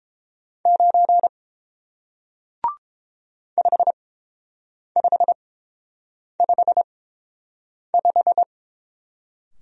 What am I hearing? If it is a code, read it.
Morse code, heard as a keyed tone, about 30 wpm, spaced as for 7 wpm, 700 Hz. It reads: 9 5555